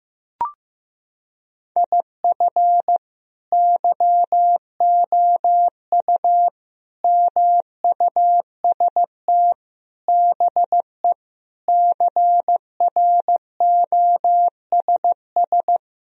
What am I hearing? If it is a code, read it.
Morse code, heard as a keyed tone, about 15 wpm, 700 Hz. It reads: IF YOU MUST BE CROSS
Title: Sentences from Top 1000 Words - Review Part 1 - 15wpm